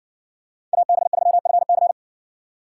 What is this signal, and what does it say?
Morse code, heard as a keyed tone, about 56 wpm, 700 Hz. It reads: AB3FC